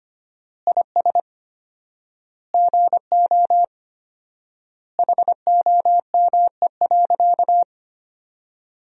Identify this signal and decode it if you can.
Morse code, heard as a keyed tone, about 25 wpm, 700 Hz. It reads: IS GO HOME.
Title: 4x Word-Spacing - Sentences from Top 200 Words - 25wpm